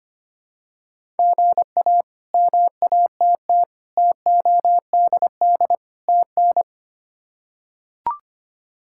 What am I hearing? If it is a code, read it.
Morse code, heard as a keyed tone, about 25 wpm, 700 Hz. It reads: GA MATT TODD TN
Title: K1USN SST QSO Practice - 25wpm